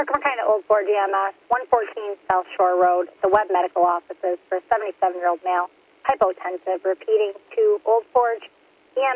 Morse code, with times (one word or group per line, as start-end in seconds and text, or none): none